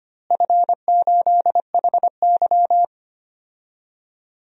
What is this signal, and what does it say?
Morse code, heard as a keyed tone, about 25 wpm, 700 Hz. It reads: F8HY
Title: DXCC-100 Call Signs - 4 Characters - Repeated 3x - 25wpm